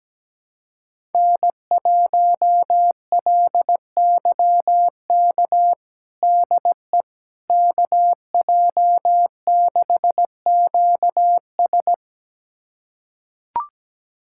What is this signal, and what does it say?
Morse code, heard as a keyed tone, about 17 wpm, 700 Hz. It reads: N1LYK DE KJ6QS
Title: USA Callsign DE Callsign - 5 Characters - 17wpm